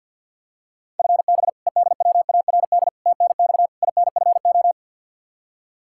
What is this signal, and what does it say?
Morse code, heard as a keyed tone, about 50 wpm, 700 Hz. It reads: FB EDWARD TNX INFO
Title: Practice QSO - Exchange 3 - More Info - 50wpm